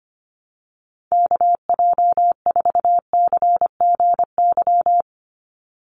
Morse code, heard as a keyed tone, about 25 wpm, 700 Hz. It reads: KJ4CGY
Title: USA Call Signs 25wpm